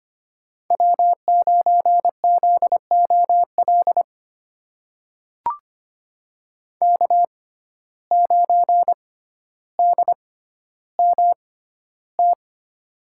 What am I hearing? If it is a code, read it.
Morse code, heard as a keyed tone, about 25 wpm, spaced as for 13 wpm, 700 Hz. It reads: W9ZOL K9DMT
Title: USA Call Signs - 5 Characters - 6x Character Spacing - 25wpm